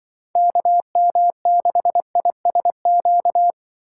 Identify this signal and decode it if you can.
Morse code, heard as a keyed tone, about 24 wpm, 700 Hz. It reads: KM6ISQ
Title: Speed Racing - USA 2x3 Call Signs 20wpm